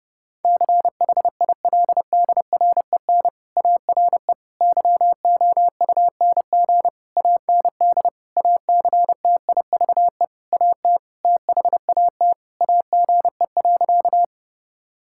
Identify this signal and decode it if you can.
Morse code, heard as a keyed tone, about 30 wpm, 700 Hz. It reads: CHILDREN ARE YOUNG AND ACTIVE AT THAT AGE.
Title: Sentences from Top 1000 Words - Review Part 6 - 30wpm